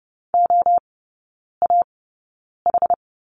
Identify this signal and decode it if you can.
Morse code, heard as a keyed tone, about 30 wpm, spaced as for 9 wpm, 700 Hz. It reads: OAH
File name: Lesson 17e - Letter u - Three Characters 30wpm